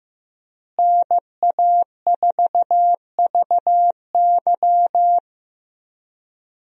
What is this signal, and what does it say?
Morse code, heard as a keyed tone, about 15 wpm, 700 Hz. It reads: NA4VY
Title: USA Call Signs - 5 Characters - Repeated 2x - 15wpm